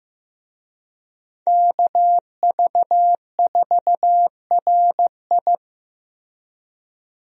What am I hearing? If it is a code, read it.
Morse code, heard as a keyed tone, about 15 wpm, 700 Hz. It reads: KV4RI
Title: USA Call Signs - 5 Characters - Repeated 3x - 15wpm